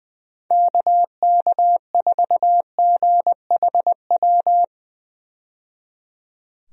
Morse code, heard as a keyed tone, about 20 wpm, 700 Hz. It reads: KK4GHW
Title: USA Call Signs - 6 Characters - 20wpm